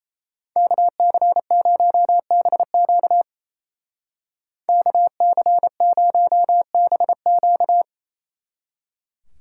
Morse code, heard as a keyed tone, about 34 wpm, 700 Hz. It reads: KC0BQ KC0BQ
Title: Speed Racing - USA 2x2 Call Signs 28wpm